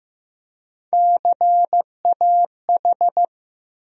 Morse code, heard as a keyed tone, about 15 wpm, 700 Hz. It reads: CAH
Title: USA Call Sign Postfixes - 3 Letters 15wpm